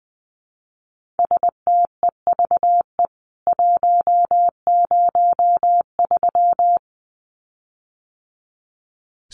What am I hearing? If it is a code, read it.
Morse code, heard as a keyed tone, about 20 wpm, 700 Hz. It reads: STEVE 103
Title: CWT Exchange 20wpm